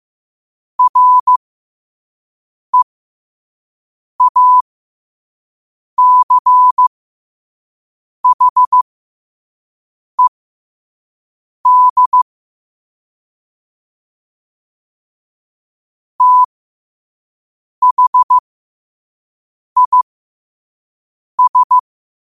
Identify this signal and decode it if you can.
Morse code, heard as a keyed tone, about 15 wpm, spaced as for 5 wpm, 1000 Hz. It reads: REACHED THIS
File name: qotdfiles / 5wpm